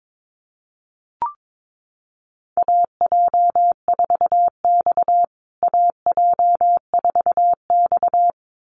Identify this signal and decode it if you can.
Morse code, heard as a keyed tone, about 22 wpm, 700 Hz. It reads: AJ4X AJ4X
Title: USA Call Signs Repeated 2x 22wpm